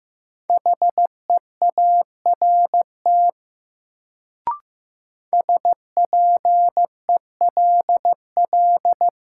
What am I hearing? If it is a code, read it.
Morse code, heard as a keyed tone, about 15 wpm, 700 Hz. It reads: HEART SPELL
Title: Sentences from Top 1200 Words - New Word Drill Alt - 15wpm